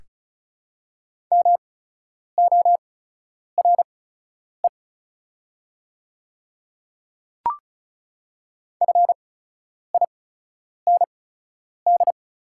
Morse code, heard as a keyed tone, about 36 wpm, spaced as for 10 wpm, 700 Hz. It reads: MORE FIND
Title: 8x Character Spacing - Top 100 Words - 35wpm